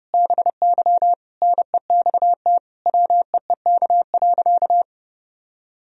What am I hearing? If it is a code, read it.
Morse code, heard as a keyed tone, about 32 wpm, 700 Hz. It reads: BY NEXT WEEK.
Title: Sentences from Top 1000 Words - Review Part 3 - 30wpm